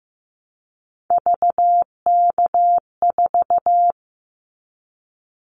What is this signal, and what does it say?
Morse code, heard as a keyed tone, about 15 wpm, 700 Hz. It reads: VK4
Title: Callsign Prefixes for Top 100 DXCC - 15wpm